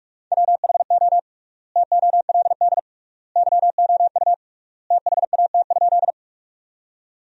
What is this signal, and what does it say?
Morse code, heard as a keyed tone, about 44 wpm, 700 Hz. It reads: WHO TOLD YOU THAT?
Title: 3x Word-Spacing - Sentences from Top 400 Words - 45wpm